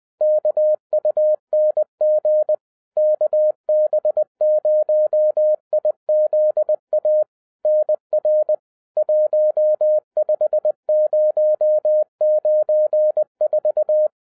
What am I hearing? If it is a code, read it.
Morse code, heard as a keyed tone, about 20 wpm, 610 Hz. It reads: KUNG KB0IZA NR 15094